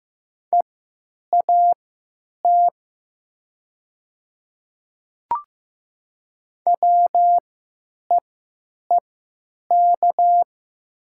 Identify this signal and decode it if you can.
Morse code, heard as a keyed tone, about 15 wpm, spaced as for 9 wpm, 700 Hz. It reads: EAT WEEK
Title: 3x Character Spacing - Top 500 Words - 15wpm